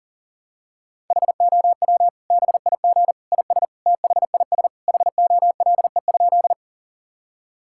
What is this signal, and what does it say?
Morse code, heard as a keyed tone, about 40 wpm, 700 Hz. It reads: HOW BIG IS THIS HOLE?
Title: Sentences from Top 1600 Words 40wpm